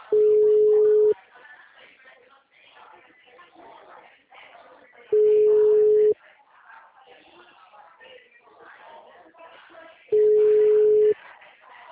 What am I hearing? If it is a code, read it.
Ringback tone (Europe, 425 Hz), with speech.